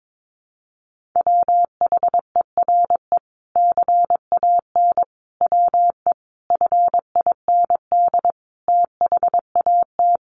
Morse code, heard as a keyed tone, about 22 wpm, 700 Hz. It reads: WHERE CAN WE FIND THAT
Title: Sentences from Top 1500 Words 22wpm